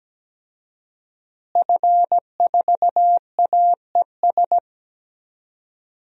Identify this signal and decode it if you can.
Morse code, heard as a keyed tone, about 17 wpm, 700 Hz. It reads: F4AES
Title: DXCC-100 Call Signs - 5 Characters - Repeated 2x - 17wpm